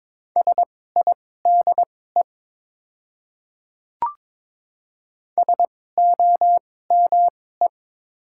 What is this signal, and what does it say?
Morse code, heard as a keyed tone, about 22 wpm, spaced as for 16 wpm, 700 Hz. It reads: SIDE SOME
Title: 2x Character Spacing - Top 100 Words - 22wpm